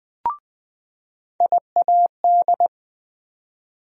Morse code, heard as a keyed tone, about 21 wpm, 700 Hz. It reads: IAD
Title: USA Call Sign Postfixes - 3 Letters 20wpm